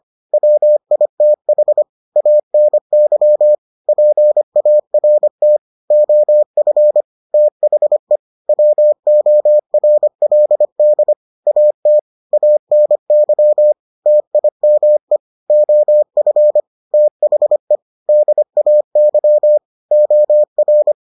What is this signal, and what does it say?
Morse code, heard as a keyed tone, about 25 wpm, 600 Hz. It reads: WITH ANY PART OF THE WORLD AT ANY TIME OF THE DAY OR